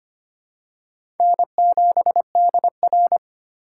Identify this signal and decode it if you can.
Morse code, heard as a keyed tone, about 25 wpm, 700 Hz. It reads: N7DR
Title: K1USN SST QSO Practice - 25wpm